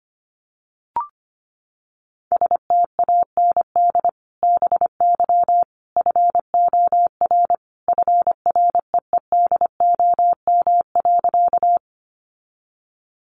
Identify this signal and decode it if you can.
Morse code, heard as a keyed tone, about 25 wpm, 700 Hz. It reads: STAND BY FOR FREEDOM.
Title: Sentences from Top 800 Words 25wpm